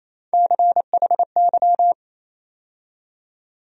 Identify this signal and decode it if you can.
Morse code, heard as a keyed tone, about 28 wpm, 700 Hz. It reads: CHY
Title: USA Call Sign Postfixes - 3 Letters 28wpm